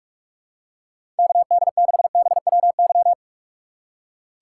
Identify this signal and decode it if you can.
Morse code, heard as a keyed tone, about 46 wpm, 700 Hz. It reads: KD6BWY
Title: USA Call Signs - 6 Characters - 2x Character Spacing - 45wpm